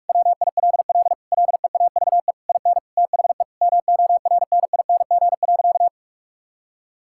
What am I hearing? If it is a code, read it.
Morse code, heard as a keyed tone, about 42 wpm, 700 Hz. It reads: WILL LEAVE IN THE MORNING.
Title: Sentences from Top 1000 Words - Review Part 6 - 45wpm